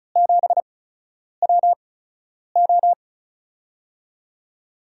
Morse code, heard as a keyed tone, about 36 wpm, spaced as for 10 wpm, 700 Hz. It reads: 7WO
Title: USA Call Signs - 4 Characters - 8x Character Spacing - 35wpm